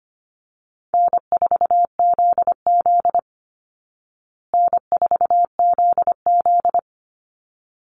Morse code, heard as a keyed tone, about 25 wpm, 700 Hz. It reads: N4ZZ N4ZZ
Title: Call Signs Repeated 3x - 25wpm